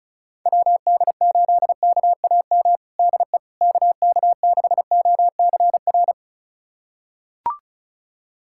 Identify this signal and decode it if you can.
Morse code, heard as a keyed tone, about 34 wpm, 700 Hz. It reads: WD8KAM DE KK6OCR
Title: USA Callsign DE Callsign 35wpm